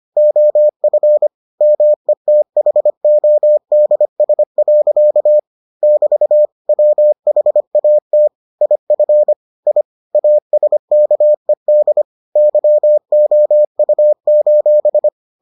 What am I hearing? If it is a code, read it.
Morse code, heard as a keyed tone, about 25 wpm, 600 Hz. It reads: OF METHODS. <BT> WHAT IF I ASKED YOU: